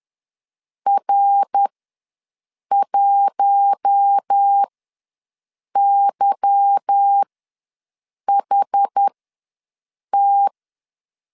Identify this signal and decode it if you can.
Morse code, heard as a keyed tone, about 10 wpm, spaced as for 6 wpm, 800 Hz. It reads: R1YHT
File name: LEZIONI / 8 YJ45 / 8 YJ45 ws 10wpm 3843